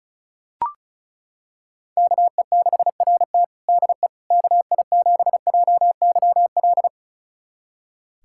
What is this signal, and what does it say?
Morse code, heard as a keyed tone, about 34 wpm, 700 Hz. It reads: KE6RT DE KI7JYL
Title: USA Callsign DE Callsign 35wpm